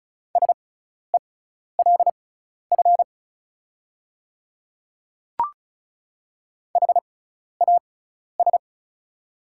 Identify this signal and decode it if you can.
Morse code, heard as a keyed tone, about 36 wpm, spaced as for 12 wpm, 690 Hz. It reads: SELF HAS
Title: 6x Character Spacing - Top 300 Words - 35wpm